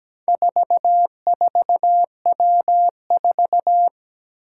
Morse code, heard as a keyed tone, about 17 wpm, 700 Hz. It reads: 44W4
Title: Four Challenging Letter-Number Combinations 17wpm